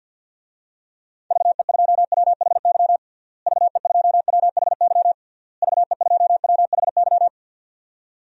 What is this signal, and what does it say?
Morse code, heard as a keyed tone, about 52 wpm, 700 Hz. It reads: VE2WHY VE2WHY VE2WHY
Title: DXCC-100 Call Signs - 6 Characters - Repeated 3x - 50wpm